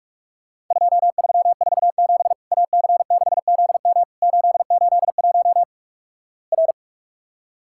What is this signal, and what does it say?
Morse code, heard as a keyed tone, about 46 wpm, 700 Hz. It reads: 2347 AC6ZM 881 R
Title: Worked All Europe DX Contest - QTC Exchange - 45wpm